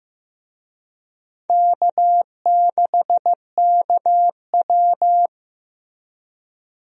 Morse code, heard as a keyed tone, about 15 wpm, 700 Hz. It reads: K6KW
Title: USA Call Signs - 4 Characters - Repeated 2x - 15wpm